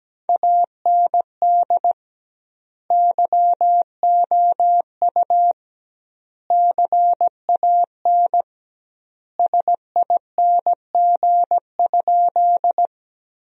Morse code, heard as a keyed tone, about 17 wpm, 700 Hz. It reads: AND YOU CAN SING?
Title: 2x Word-Spacing - Sentences from Top 400 Words - 17wpm